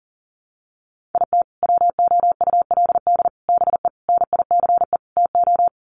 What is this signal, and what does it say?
Morse code, heard as a keyed tone, about 40 wpm, 700 Hz. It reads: IT WOULD BE NICE TO
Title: Sets of 5 Words Encore2 40wpm